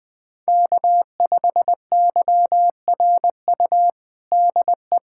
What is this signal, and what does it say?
Morse code, heard as a keyed tone, about 20 wpm, 700 Hz. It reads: K5YRU DE
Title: USA Callsign DE Callsign 20wpm